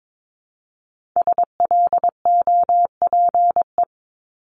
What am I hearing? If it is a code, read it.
Morse code, heard as a keyed tone, about 22 wpm, 700 Hz. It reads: SLOPE
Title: Multi-letter Phonemes Repeated 3x 22wpm